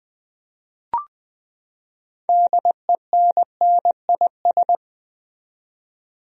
Morse code, heard as a keyed tone, about 20 wpm, 700 Hz. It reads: DENNIS
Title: USA Common Names - 20wpm